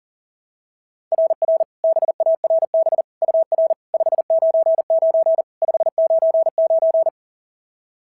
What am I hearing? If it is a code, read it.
Morse code, heard as a keyed tone, about 37 wpm, 690 Hz. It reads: RR BARB UR 599 599
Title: SOTA QSO Practice 40wpm